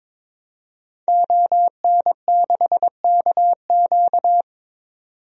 Morse code, heard as a keyed tone, about 22 wpm, 700 Hz. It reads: ON6KQ
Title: DXCC-100 Call Signs - 5 Characters - 22wpm